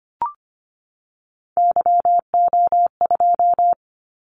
Morse code, heard as a keyed tone, about 25 wpm, 700 Hz. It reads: YO2
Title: Callsign Prefixes for Top 25 DXCC - 25wpm